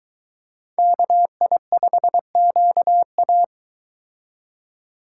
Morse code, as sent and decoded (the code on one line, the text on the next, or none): -.- .. ..... --.- .-
KI5QA